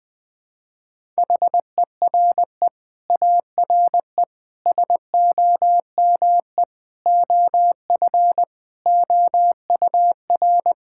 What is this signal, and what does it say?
Morse code, heard as a keyed tone, about 20 wpm, 700 Hz. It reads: HERE ARE SOME OF OUR
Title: Sentences with 9-Word Phrases - Spoken Beginning - 20wpm